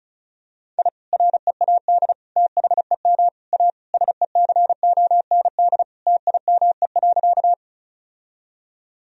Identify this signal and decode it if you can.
Morse code, heard as a keyed tone, about 36 wpm, 700 Hz. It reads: I READ THEM A SECOND TIME.
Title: Sentences from Top 1000 Words - Review Part 1 - 35wpm